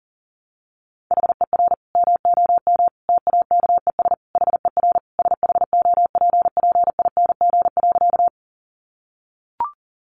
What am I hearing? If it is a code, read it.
Morse code, heard as a keyed tone, about 37 wpm, 700 Hz. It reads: HER MOM TAKES HER SHOPPING.
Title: Additional Easy Phrases 40wpm